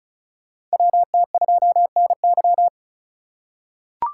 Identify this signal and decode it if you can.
Morse code, heard as a keyed tone, about 34 wpm, 700 Hz. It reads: WT2NY
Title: USA Call Signs - 5 Characters - 4x Character Spacing - 35wpm